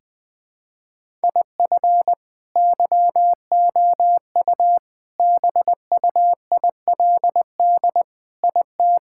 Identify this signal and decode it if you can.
Morse code, heard as a keyed tone, about 20 wpm, 700 Hz. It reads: IF YOU BUILD IT
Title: Sentences with 7-Word Phrases - Spoken Beginning - 20wpm